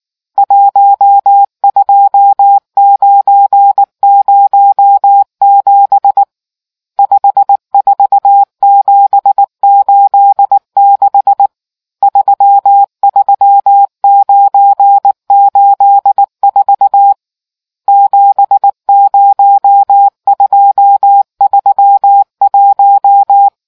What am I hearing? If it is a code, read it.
Morse code, heard as a keyed tone, about 19 wpm, 800 Hz. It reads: 12907 54786 33984 70231